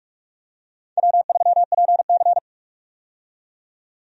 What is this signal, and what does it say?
Morse code, heard as a keyed tone, about 46 wpm, 700 Hz. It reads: W3PC
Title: USA Call Signs - 4 Characters - 45wpm